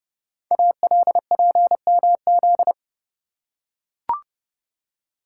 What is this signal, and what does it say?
Morse code, heard as a keyed tone, about 30 wpm, 700 Hz. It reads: ALPMZ
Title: Code Groups - 5 Letter:Numbers - 30wpm